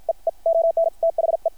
Morse code, heard as a keyed tone, about 50 wpm, spaced as for 42 wpm, 660 Hz. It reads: EE ONTHE
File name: archived / CRSnet